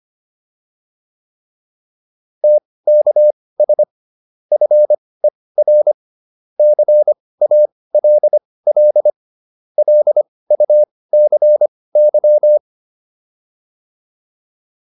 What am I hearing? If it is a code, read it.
Morse code, heard as a keyed tone, about 25 wpm, spaced as for 18 wpm, 600 Hz. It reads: TKS FER CALL LUCY
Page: intermediate-practice-files